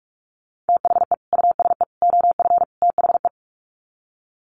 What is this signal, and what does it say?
Morse code, heard as a keyed tone, about 44 wpm, 700 Hz. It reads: THE USE OF THE
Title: Sets of 4 Words 45wpm